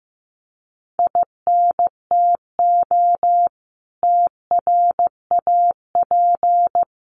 Morse code, heard as a keyed tone, about 15 wpm, 700 Hz. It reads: INTO TRAP
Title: Common QSO Phrases 15wpm